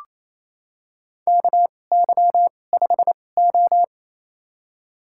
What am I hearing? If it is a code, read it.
Morse code, heard as a keyed tone, about 28 wpm, spaced as for 20 wpm, 700 Hz. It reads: KY5O
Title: USA Call Signs - 4 Characters - 2x Character Spacing - 28wpm